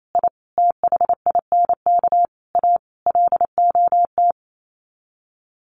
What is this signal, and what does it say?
Morse code, heard as a keyed tone, about 28 wpm, 700 Hz. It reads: I THINK A LOT
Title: Sets of 5 Words Encore2 28wpm